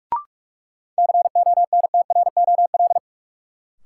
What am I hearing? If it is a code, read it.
Morse code, heard as a keyed tone, about 46 wpm, 700 Hz. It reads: CONTROL
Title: Sentences from Top 1000 Words - Review Part 2 - New Word Drill - 45wpm